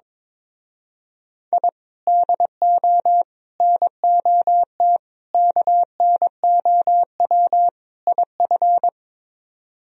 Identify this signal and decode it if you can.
Morse code, heard as a keyed tone, about 22 wpm, 700 Hz. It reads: I DO NOT KNOW IF
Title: Speed-Racing - Sets of 5 Words 22wpm